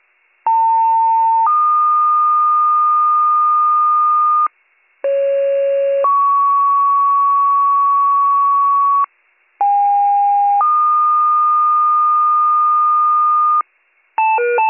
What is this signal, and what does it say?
Two-tone page, heard as a one-second tone, then a three-second tone, about 900 then 1220 Hz, about 550 then 1120 Hz, about 810 then 1220 Hz. It is heard from 0.5 to 13.6 s.